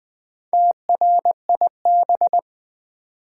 Morse code, heard as a keyed tone, about 20 wpm, 700 Hz. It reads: TRIB
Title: Build A Word - 1 Letter At A Time - 3 to 5-Letter Words - 20wpm